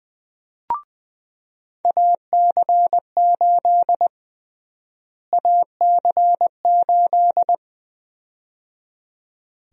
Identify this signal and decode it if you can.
Morse code, heard as a keyed tone, about 20 wpm, 700 Hz. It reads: AC8 AC8
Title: USA Call Sign Prefixes - Repeated 2x - 20wpm